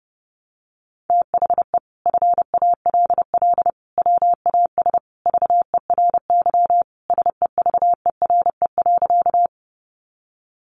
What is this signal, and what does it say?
Morse code, heard as a keyed tone, about 30 wpm, 700 Hz. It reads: THE FALL WAS VERY SEVERE.
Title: Sentences from Top 1400 Words 30wpm